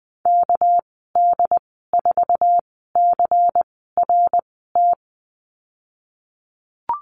Morse code, heard as a keyed tone, about 20 wpm, spaced as for 14 wpm, 700 Hz. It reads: KD4CRT
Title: 245 Lesson 39g - Letter z - Call Signs 20wpm